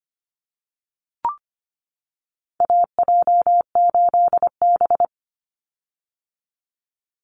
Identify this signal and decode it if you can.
Morse code, heard as a keyed tone, about 25 wpm, 700 Hz. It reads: AJ8B
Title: Call Signs - 25wpm